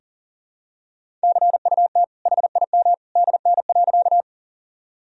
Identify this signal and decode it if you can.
Morse code, heard as a keyed tone, about 40 wpm, 700 Hz. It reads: CUT HIM DN.
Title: Sentences from Top 300 Words and Abbreviations 40wpm